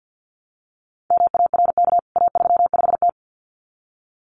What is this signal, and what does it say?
Morse code, heard as a keyed tone, about 50 wpm, 700 Hz. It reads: MARK A35T
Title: CWT Exchange 50wpm